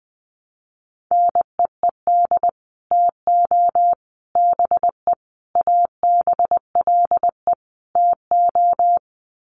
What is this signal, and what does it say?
Morse code, heard as a keyed tone, about 20 wpm, 700 Hz. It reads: NEED TO BE ABLE TO
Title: Sets of 5 Words Encore 20wpm